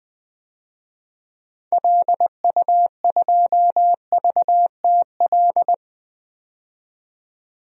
Morse code, heard as a keyed tone, about 20 wpm, 700 Hz. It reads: LU2VTL
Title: DXCC-100 Call Signs - 6 Characters - 20wpm